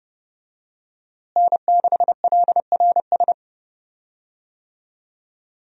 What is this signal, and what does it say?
Morse code, heard as a keyed tone, about 30 wpm, 700 Hz. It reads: N6LRS
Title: USA Call Signs - 5 Characters - Repeated 3x - 30wpm